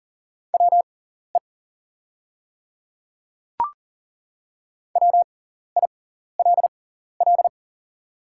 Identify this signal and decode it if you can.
Morse code, heard as a keyed tone, about 40 wpm, spaced as for 14 wpm, 700 Hz. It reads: WE WILL
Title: 6x Character Spacing - Top 100 Words - 40wpm